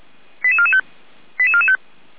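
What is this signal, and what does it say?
Five-tone selective call (ZVEI), heard as quick runs of five tones, tones of about 70 ms. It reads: 88445 88445
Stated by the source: ZVEI five-tone selective call received via SDR, from the audio archive of the Feuerwehr Orlamünde fire brigade.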